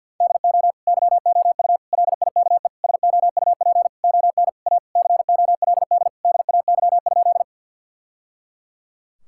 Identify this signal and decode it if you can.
Morse code, heard as a keyed tone, about 50 wpm, 700 Hz. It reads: DO YOU LIKE SOUP ON A COLD DAY?
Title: Sentences from Top 1000 Words - Review Part 6 - 50wpm